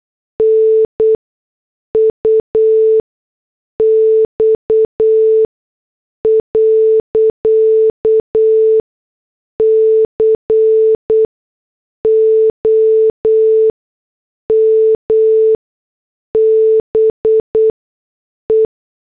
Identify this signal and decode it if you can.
Morse code, heard as a keyed tone, about 8 wpm, spaced as for 6 wpm, 440 Hz. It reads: NUX.COM BE
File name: morse / headlines